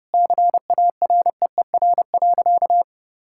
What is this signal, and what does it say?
Morse code, heard as a keyed tone, about 30 wpm, 700 Hz. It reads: CAREER.
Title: Sentences from Top 1000 Words - Review Part 6 - 30wpm